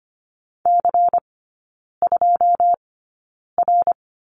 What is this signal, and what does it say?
Morse code, heard as a keyed tone, about 25 wpm, spaced as for 9 wpm, 700 Hz. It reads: C2R
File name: course / 25:10wpm / 093 Lesson 16e - Letter c - Three Characters 25wpm